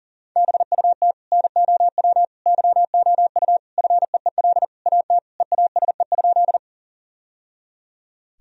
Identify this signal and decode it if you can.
Morse code, heard as a keyed tone, about 40 wpm, 700 Hz. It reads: BUT NOW YOU FEEL AT EASE?
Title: Sentences from Top 1000 Words - Review Part 1 - 40wpm